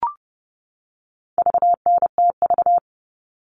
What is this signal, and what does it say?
Morse code, heard as a keyed tone, about 30 wpm, 700 Hz. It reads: VNTV
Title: All Cut Numbers - 4-Digit 30wpm